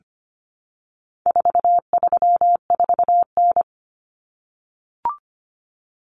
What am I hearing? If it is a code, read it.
Morse code, heard as a keyed tone, about 25 wpm, 700 Hz. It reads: 434N
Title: Common Cut Numbers - 4-Digit 25wpm